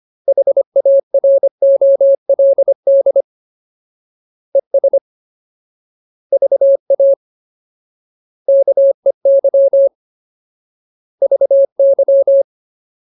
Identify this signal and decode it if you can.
Morse code, heard as a keyed tone, about 25 wpm, 550 Hz. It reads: HAROLD ES VA KEY VY
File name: Ultimate Word List / 25WPM Ultimate